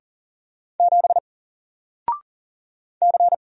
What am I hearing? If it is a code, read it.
Morse code, heard as a keyed tone, about 40 wpm, 700 Hz. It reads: 7 C
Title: Single Letter-Number 40wpm